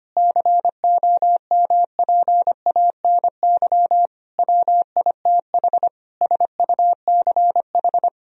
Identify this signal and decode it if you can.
Morse code, heard as a keyed tone, about 25 wpm, 700 Hz. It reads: COMPANY WITH SUCH
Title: Sentences from Top 1000 Words - Review Part 3 - 25wpm